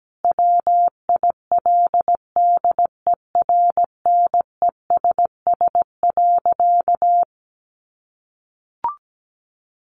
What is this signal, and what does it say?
Morse code, heard as a keyed tone, about 17 wpm, 700 Hz. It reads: WILDERNESS.
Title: A Sound Of Thunder 17wpm